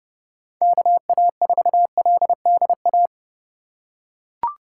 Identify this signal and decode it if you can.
Morse code, heard as a keyed tone, about 30 wpm, 700 Hz. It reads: KA4LDA